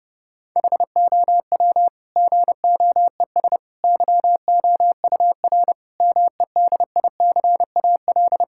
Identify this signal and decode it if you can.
Morse code, heard as a keyed tone, about 30 wpm, 700 Hz. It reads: HOW GOES YOUR MEDICAL
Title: Sentences from Top 1000 Words - Review Part 5 - 30wpm